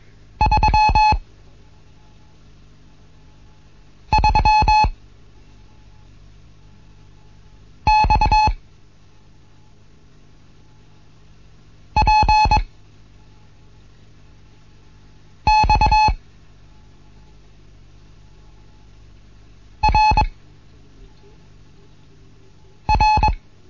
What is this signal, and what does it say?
Morse code, heard as a keyed tone, about 21 wpm, 860 Hz. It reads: 3 3 X P X R R
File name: Morsen / Kurs A 15